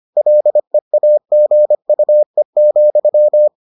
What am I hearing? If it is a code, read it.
Morse code, heard as a keyed tone, about 25 wpm, 600 Hz. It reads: LEAGUE,